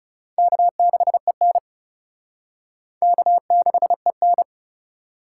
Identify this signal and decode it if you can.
Morse code, heard as a keyed tone, about 37 wpm, 700 Hz. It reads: K6EN K6EN
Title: Speed Racing - USA 1x2 Call Signs 30wpm